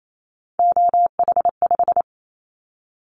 Morse code, heard as a keyed tone, about 28 wpm, 700 Hz. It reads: OH5
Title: Callsign Prefixes for Top 50 DXCC - 28wpm